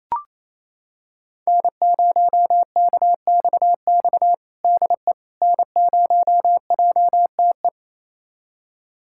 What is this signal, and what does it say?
Morse code, heard as a keyed tone, about 28 wpm, 700 Hz. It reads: N0KXX DE N0JTE
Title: USA Callsign DE Callsign - 5 Characters - 28wpm